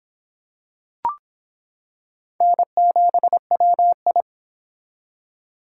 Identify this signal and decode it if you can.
Morse code, heard as a keyed tone, about 25 wpm, 700 Hz. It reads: N7WI